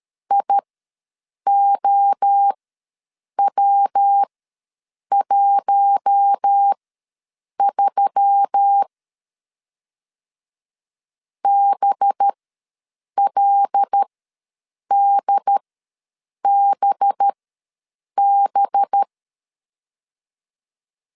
Morse code, heard as a keyed tone, about 13 wpm, spaced as for 7 wpm, 800 Hz. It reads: IOW13 BLDBB
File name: 5 LB ws 12wpm